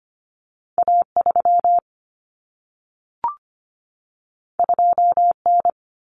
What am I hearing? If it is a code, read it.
Morse code, heard as a keyed tone, about 25 wpm, 700 Hz. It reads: A3 2N
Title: Common Cut Numbers - 2-Digit 25wpm